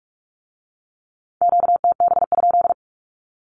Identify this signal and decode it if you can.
Morse code, heard as a keyed tone, about 46 wpm, 700 Hz. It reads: QTB?
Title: Q-Codes with Spoken Meaning - 45wpm